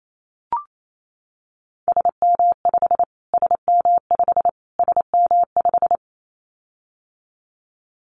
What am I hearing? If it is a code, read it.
Morse code, heard as a keyed tone, about 28 wpm, 700 Hz. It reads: SM5 SM5 SM5
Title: Callsign Prefixes for Top 25 DXCC - Repeated 3x - 28wpm